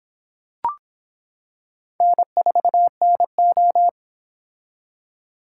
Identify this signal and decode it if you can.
Morse code, heard as a keyed tone, about 25 wpm, 700 Hz. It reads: N4NO